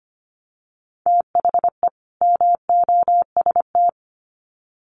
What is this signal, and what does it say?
Morse code, heard as a keyed tone, about 25 wpm, 700 Hz. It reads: THE MOST
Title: Sets of 2 Words 25wpm